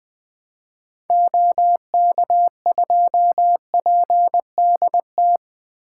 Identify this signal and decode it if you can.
Morse code, heard as a keyed tone, about 20 wpm, 700 Hz. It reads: OK2PDT
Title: 2023 SOTA Chaser Call Signs - 20wpm